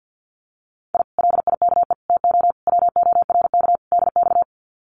Morse code, heard as a keyed tone, about 48 wpm, 700 Hz. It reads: I LIKE TO WORK DX
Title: Common QSO Phrases Encore 50wpm